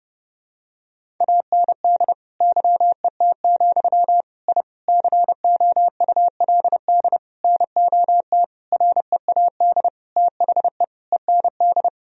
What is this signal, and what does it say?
Morse code, heard as a keyed tone, about 30 wpm, 700 Hz. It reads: AND YET, I COULD NOT READ THE END
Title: Sentences from Top 1000 Words - Review Part 1 - 30wpm